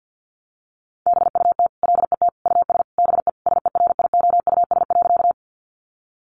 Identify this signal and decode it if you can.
Morse code, heard as a keyed tone, about 49 wpm, 700 Hz. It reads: BUT LET US BE SERIOUS.
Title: Sentences from Top 1100 Words 50wpm